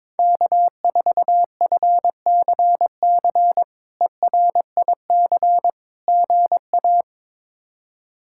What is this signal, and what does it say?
Morse code, heard as a keyed tone, about 22 wpm, 700 Hz. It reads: K4FCC ERIC GA
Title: K1USN SST QSO Practice - 22wpm